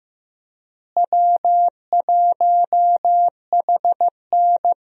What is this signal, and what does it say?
Morse code, heard as a keyed tone, about 15 wpm, 700 Hz. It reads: W1HN
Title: USA Call Signs - 4 Characters - Repeated 3x - 15wpm